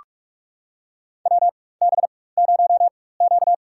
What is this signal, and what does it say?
Morse code, heard as a keyed tone, about 46 wpm, spaced as for 21 wpm, 690 Hz. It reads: WB0Q
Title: USA Call Signs - 4x Character Spacing - 45wpm